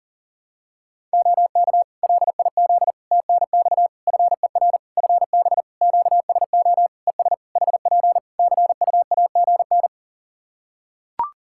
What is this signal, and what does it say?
Morse code, heard as a keyed tone, about 40 wpm, 700 Hz. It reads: OK LIZ TNX FER FB QSO ES HP CUAGN
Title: Practice QSO - Exchange 4 - Ending - 40wpm